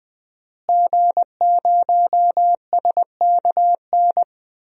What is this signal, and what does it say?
Morse code, heard as a keyed tone, about 20 wpm, 700 Hz. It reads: G0SKN
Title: DXCC-100 Call Signs - 5 Characters - Repeated 2x - 20wpm